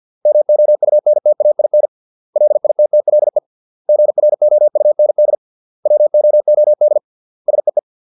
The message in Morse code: MOUNTAIN LITTLE GROUND WOOD SEE

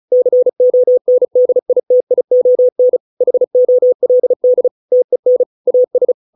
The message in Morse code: CONDITION HOLD TEN AS

